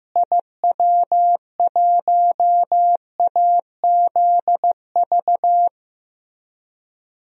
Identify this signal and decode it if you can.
Morse code, heard as a keyed tone, about 15 wpm, 700 Hz. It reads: IW1AZV
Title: DXCC-100 Call Signs - 6 Characters - Repeated 3x - 15wpm